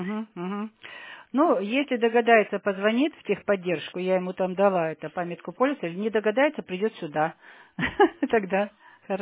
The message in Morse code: none

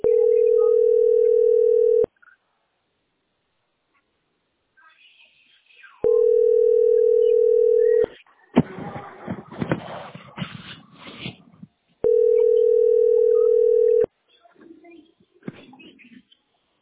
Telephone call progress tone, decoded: ringback tone

